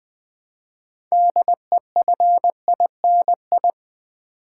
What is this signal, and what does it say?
Morse code, heard as a keyed tone, about 20 wpm, 700 Hz. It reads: DEFINI
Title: Build A Word - 2 Letters At A Time - 8-Letter Words - 20wpm